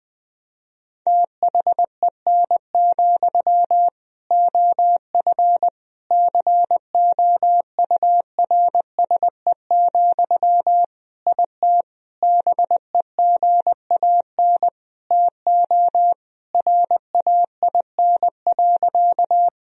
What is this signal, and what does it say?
Morse code, heard as a keyed tone, about 21 wpm, 700 Hz. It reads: THEN, OF COURSE, IT BEGAN TO RAIN.
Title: Sentences from Top 1000 Words - Review Part 1 - 20wpm